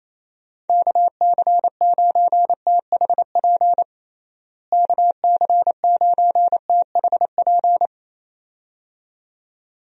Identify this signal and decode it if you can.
Morse code, heard as a keyed tone, about 27 wpm, 700 Hz. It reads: KC9THP KC9THP
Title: USA Call Signs - 6 Characters - Repeated 2x - 28wpm